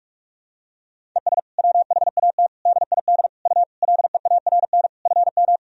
Morse code, heard as a keyed tone, about 46 wpm, 700 Hz. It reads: ES WHAT DID U LEARN FM
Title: Sentences from Top 300 Words and Abbreviations 45wpm